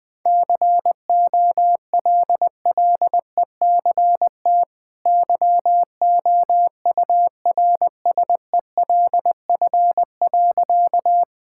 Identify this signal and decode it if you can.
Morse code, heard as a keyed tone, about 20 wpm, 700 Hz. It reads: COLLECT YOURSELF.
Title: Sentences from Top 1600 Words 20wpm